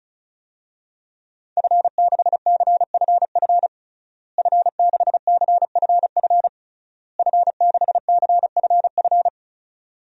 Morse code, heard as a keyed tone, about 36 wpm, 700 Hz. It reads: F6CFF F6CFF F6CFF
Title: DXCC-100 Call Signs - 5 Characters - Repeated 3x - 35wpm